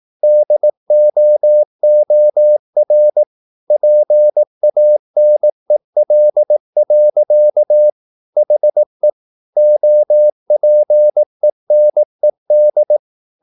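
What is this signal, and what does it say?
Morse code, heard as a keyed tone, about 18 wpm, 600 Hz. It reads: DOOR PANEL. HE OPENED
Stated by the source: Morse practice at 18 wpm, English text, from CW med Gnister.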